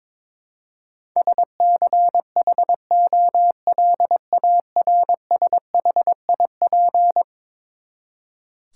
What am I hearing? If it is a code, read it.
Morse code, heard as a keyed tone, about 21 wpm, 700 Hz. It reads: SCHOLARSHIP
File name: practice / 22wpm / Eleven Letter Words - 22wpm